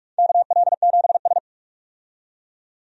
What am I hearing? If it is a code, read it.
Morse code, heard as a keyed tone, about 44 wpm, 700 Hz. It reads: KL7S
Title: DXCC-100 Call Signs - 4 Characters - 45wpm